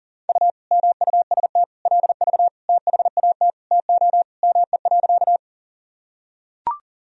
Morse code, heard as a keyed tone, about 40 wpm, 700 Hz. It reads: U MUST LV THAT TO ME.
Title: Sentences from Top 400 Words and Abbreviations 40wpm